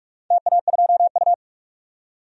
Morse code, heard as a keyed tone, about 44 wpm, 700 Hz. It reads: TA2U
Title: DXCC-100 Call Signs - 4 Characters - 45wpm